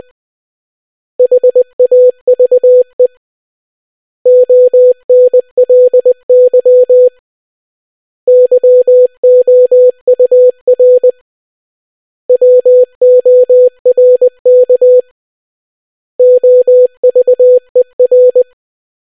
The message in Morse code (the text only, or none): HAVE ONLY YOUR WORK OVER